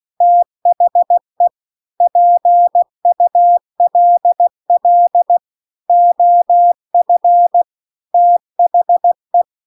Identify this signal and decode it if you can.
Morse code, heard as a keyed tone, about 16 wpm, 700 Hz. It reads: THE PULL OF THE
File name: CW-The-Big-Twoheaded-River-2-16WPM